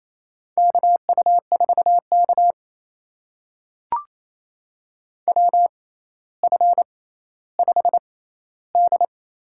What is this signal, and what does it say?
Morse code, heard as a keyed tone, about 28 wpm, spaced as for 14 wpm, 700 Hz. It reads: KU4K WF5D
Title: USA Call Signs - 4 Characters - 6x Character Spacing - 28wpm